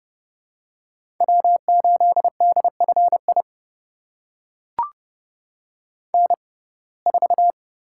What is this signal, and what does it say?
Morse code, heard as a keyed tone, about 30 wpm, spaced as for 22 wpm, 700 Hz. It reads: W8DFI N4
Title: USA Call Signs - 5 Characters - 6x Character Spacing - 30wpm